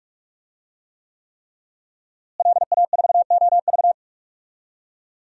Morse code, heard as a keyed tone, about 46 wpm, 700 Hz. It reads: LA4OV
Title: DXCC-100 Call Signs - 5 Characters - Repeated 3x - 45wpm